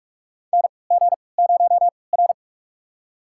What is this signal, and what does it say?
Morse code, heard as a keyed tone, about 41 wpm, spaced as for 25 wpm, 700 Hz. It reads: NG0R
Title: USA Call Signs - 3x Character Spacing - 45wpm